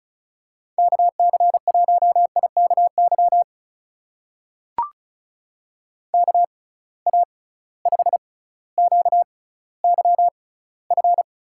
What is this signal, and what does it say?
Morse code, heard as a keyed tone, about 34 wpm, spaced as for 18 wpm, 700 Hz. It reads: KC1IKY KA5QYF
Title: USA Call Signs - 6 Characters - 6x Character Spacing - 35wpm